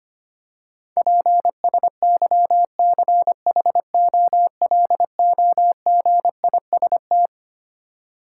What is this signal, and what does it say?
Morse code, heard as a keyed tone, about 25 wpm, 700 Hz. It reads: PSYCHOLOGIST
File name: Twelve Letter Words - 25wpm